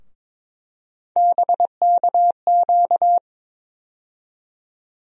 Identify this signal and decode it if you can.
Morse code, heard as a keyed tone, about 22 wpm, 700 Hz. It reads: BKQ
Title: USA Call Sign Postfixes - 3 Letters 22wpm